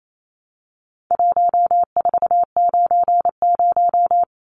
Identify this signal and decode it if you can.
Morse code, heard as a keyed tone, about 28 wpm, 700 Hz. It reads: 1490
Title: Numbers - 4-Digit 28wpm